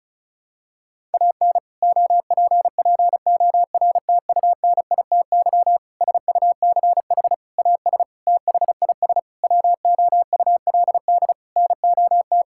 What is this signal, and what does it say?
Morse code, heard as a keyed tone, about 36 wpm, 700 Hz. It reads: AN OPPORTUNITY SUCH AS THIS WOULD NOT
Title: Sentences from Top 1600 Words 35wpm